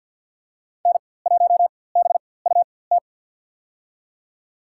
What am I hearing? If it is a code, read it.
Morse code, heard as a keyed tone, about 53 wpm, spaced as for 24 wpm, 700 Hz. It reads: N1BUT
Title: USA Call Signs - 5 Characters - 4x Character Spacing - 50wpm